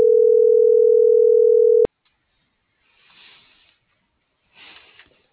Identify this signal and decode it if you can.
Ringback tone (North America), 440 + 480 Hz.